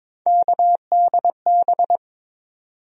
Morse code, heard as a keyed tone, about 22 wpm, 700 Hz. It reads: KDB
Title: USA Call Sign Postfixes - 3 Letters 22wpm